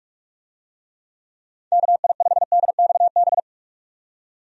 Morse code, heard as a keyed tone, about 42 wpm, 700 Hz. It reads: KI5DXB
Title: USA Call Signs - 6 Characters - 45wpm